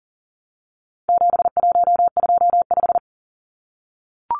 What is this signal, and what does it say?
Morse code, heard as a keyed tone, about 40 wpm, 700 Hz. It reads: 7125
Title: Numbers - 4-Digit 40wpm